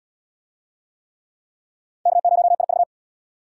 Morse code, heard as a keyed tone, about 77 wpm, 700 Hz. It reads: K0IV